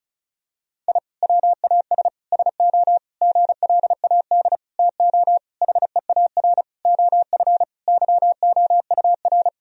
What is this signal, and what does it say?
Morse code, heard as a keyed tone, about 36 wpm, 700 Hz. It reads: I WAS SO GLAD TO HEAR OF YOUR
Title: Sentences from Top 1000 Words - Review Part 4 - 35wpm